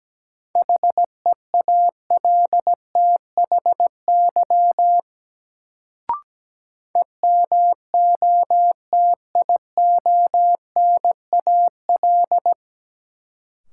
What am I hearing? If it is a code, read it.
Morse code, heard as a keyed tone, about 17 wpm, 700 Hz. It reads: HEALTHY EMOTIONAL
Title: Sentences from Top 1000 Words - Review Part 5 - New Word Drill - 17wpm